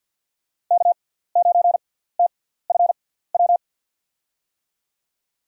Morse code, heard as a keyed tone, about 53 wpm, spaced as for 17 wpm, 700 Hz. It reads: K9TFW
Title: USA Call Signs - 6x Character Spacing - 50wpm